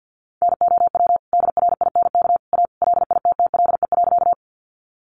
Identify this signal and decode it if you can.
Morse code, heard as a keyed tone, about 50 wpm, 700 Hz. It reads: NOW DRINK A LITTLE.
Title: Sentences from Top 1500 Words 50wpm